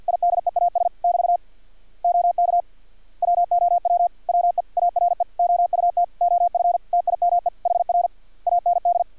Morse code, heard as a keyed tone, about 50 wpm, 700 Hz. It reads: IDEAS <BT> OK WOW WE ARE OUT OF TIME HR AND
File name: archived / CRSnet